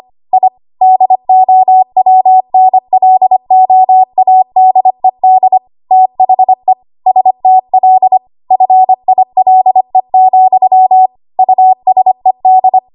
Morse code, heard as a keyed tone, about 25 wpm, 750 Hz. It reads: I DOWNLOADED THE STL FILE, USED
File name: Archive / 25 WPM / 25WPM